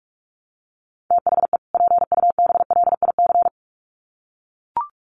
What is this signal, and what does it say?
Morse code, heard as a keyed tone, about 44 wpm, 700 Hz. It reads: THE PUBLIC
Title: Sets of 2 Words 45wpm